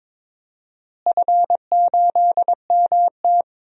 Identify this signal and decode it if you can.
Morse code, heard as a keyed tone, about 22 wpm, 700 Hz. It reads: F8MT
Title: DXCC-100 Call Signs - 4 Characters - Repeated 2x - 22wpm